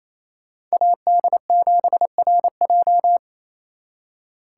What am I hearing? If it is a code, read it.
Morse code, heard as a keyed tone, about 36 wpm, 700 Hz. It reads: AD7RJ